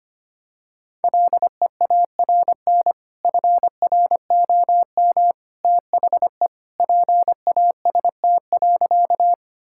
Morse code, heard as a keyed tone, about 25 wpm, 700 Hz. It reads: LEARN FROM THE PAST.
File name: practice / 25wpm / Sentences with 2-Word Phrases - 25wpm